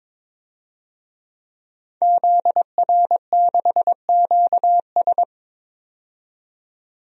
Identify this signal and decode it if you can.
Morse code, heard as a keyed tone, about 22 wpm, 700 Hz. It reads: ZR6QS
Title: DXCC-100 Call Signs - 5 Characters - 22wpm